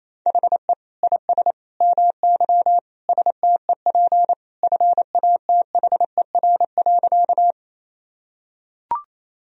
Morse code, heard as a keyed tone, about 28 wpm, 700 Hz. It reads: HE IS MY STEP FATHER.
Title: Sentences from Top 1000 Words - Review Part 2 - 28wpm